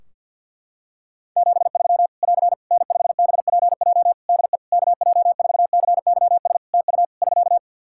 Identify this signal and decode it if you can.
Morse code, heard as a keyed tone, about 48 wpm, 700 Hz. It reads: 73 <AR> N5BPJ DE KJ4CYS TU <SK>
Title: Practice QSO - Exchange 4 - Ending - 50wpm